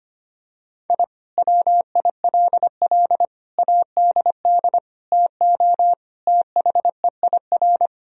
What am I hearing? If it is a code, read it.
Morse code, heard as a keyed tone, about 25 wpm, 700 Hz. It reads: I WILL ADD TO THEIR
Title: Sentences from Top 1000 Words - Review Part 1 - 25wpm